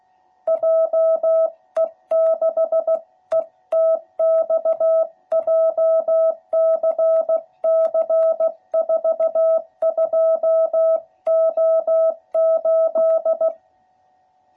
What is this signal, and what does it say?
Morse code, heard as a keyed tone, about 15 wpm, 660 Hz. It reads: JE6ETX JCC42O8
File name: jh8cbh / 2021cw / cw12b